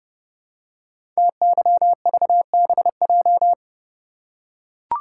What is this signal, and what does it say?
Morse code, heard as a keyed tone, about 30 wpm, 700 Hz. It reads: TYVBJ
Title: Code Groups - 5 Letters - 30wpm